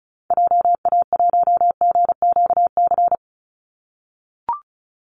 Morse code, heard as a keyed tone, about 34 wpm, 700 Hz. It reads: JA1GQC
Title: Call Signs - 35wpm